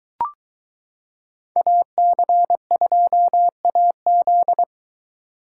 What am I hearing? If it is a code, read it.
Morse code, heard as a keyed tone, about 22 wpm, 700 Hz. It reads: AC2AZ